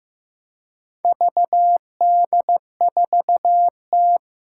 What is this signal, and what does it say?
Morse code, heard as a keyed tone, about 15 wpm, 700 Hz. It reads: VD4T
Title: Four Challenging Letter-Number Combinations 15wpm